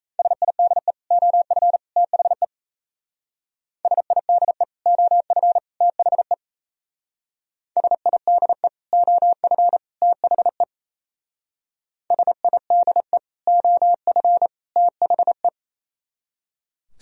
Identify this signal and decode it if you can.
Morse code, heard as a keyed tone, about 35 wpm, 700 Hz. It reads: SIDE OF THE SIDE OF THE SIDE OF THE SIDE OF THE
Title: Speed-Racing - Sets of 3 Words 28wpm